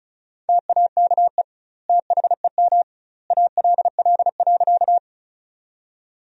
Morse code, heard as a keyed tone, about 34 wpm, 700 Hz. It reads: TAKE THEM ALL.
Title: 2x Word-Spacing - Sentences from Top 200 Words - 35wpm